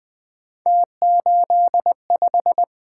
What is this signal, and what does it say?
Morse code, heard as a keyed tone, about 20 wpm, 700 Hz. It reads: T85
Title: Three Letter-Number Combinations - 20wpm